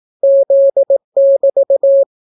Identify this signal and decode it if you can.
Morse code, heard as a keyed tone, about 18 wpm, 550 Hz. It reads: Z<BT>